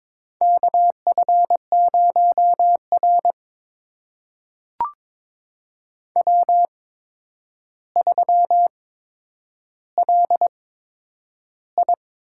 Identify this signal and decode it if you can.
Morse code, heard as a keyed tone, about 22 wpm, spaced as for 9 wpm, 700 Hz. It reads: KF0R W3LI
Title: USA Call Signs - 4 Characters - 8x Character Spacing - 22wpm